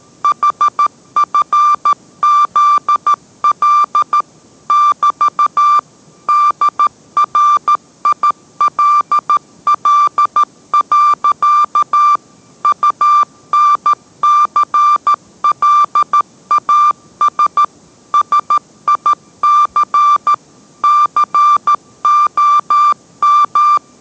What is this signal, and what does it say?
Morse code, heard as a keyed tone, about 13 wpm, 1180 Hz. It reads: HFZL <BT> DRILL. UNCLAS SIC COM